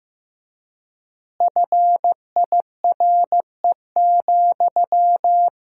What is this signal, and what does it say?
Morse code, heard as a keyed tone, about 15 wpm, 700 Hz. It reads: FIRE,
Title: 4x Word-Spacing - Sentences from Top 400 Words - 15wpm